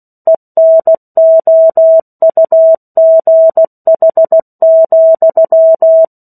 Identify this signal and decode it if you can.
Morse code, heard as a keyed tone, about 16 wpm, 650 Hz. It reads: ENOUGH,
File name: 03-Chapter-03 16 WPM